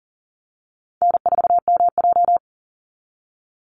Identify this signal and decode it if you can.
Morse code, heard as a keyed tone, about 40 wpm, 700 Hz. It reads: N4MJ
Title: NA SOTA Chaser Callsigns 40wpm